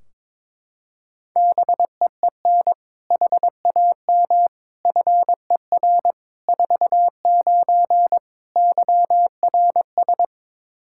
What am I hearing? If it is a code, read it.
Morse code, heard as a keyed tone, about 22 wpm, 700 Hz. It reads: BEEN HAM FER 49 YRS
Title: K1USN SST QSO Practice 22wpm